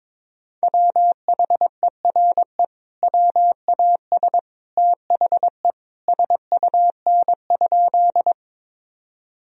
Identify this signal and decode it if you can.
Morse code, heard as a keyed tone, about 22 wpm, 700 Hz. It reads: WHERE WAS THE SUN?
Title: Sentences from Top 1000 Words - Review Part 1 - 22wpm